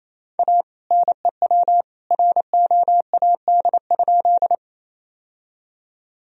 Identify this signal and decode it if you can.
Morse code, heard as a keyed tone, about 28 wpm, 700 Hz. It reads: A NEW ROAD?
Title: Sentences from Top 1000 Words - Review Part 2 - 28wpm